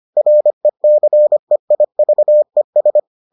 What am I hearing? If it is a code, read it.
Morse code, heard as a keyed tone, about 25 wpm, 600 Hz. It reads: RECEIVES